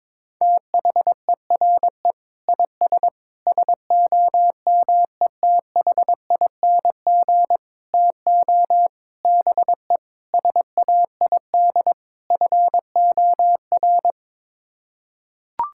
Morse code, heard as a keyed tone, about 22 wpm, 700 Hz. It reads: THERE IS SOMETHING TO BE SAID FOR
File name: practice / 22wpm / Sets of 7 Words - 22wpm